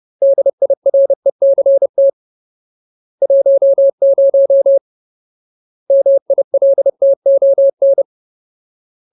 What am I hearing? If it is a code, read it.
Morse code, heard as a keyed tone, about 30 wpm, 550 Hz. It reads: DIRECT 10 MILTON
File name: Ultimate Word List / 30WPM Ultimate